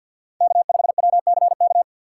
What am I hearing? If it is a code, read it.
Morse code, heard as a keyed tone, about 52 wpm, 700 Hz. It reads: K5WCK
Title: USA Call Signs - 5 Characters - 2x Character Spacing - 50wpm